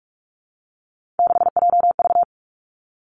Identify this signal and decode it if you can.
Morse code, heard as a keyed tone, about 46 wpm, 700 Hz. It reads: 6JV
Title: Three Challenging Letter-Number Combinations 45wpm